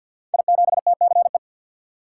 Morse code, heard as a keyed tone, about 49 wpm, 700 Hz. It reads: I7TCE
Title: Code Groups - 5 Letter:Numbers - 50wpm